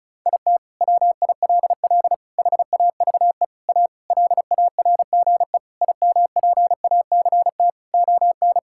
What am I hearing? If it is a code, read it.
Morse code, heard as a keyed tone, about 34 wpm, 700 Hz. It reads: IT WILL HAVE A LARGE IMPACT ON
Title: Sentences from Top 1000 Words - Review Part 6 - 35wpm